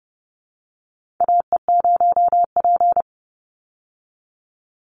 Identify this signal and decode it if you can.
Morse code, heard as a keyed tone, about 30 wpm, 700 Hz. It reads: AE0P
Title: Call Signs Repeated 3x - 30wpm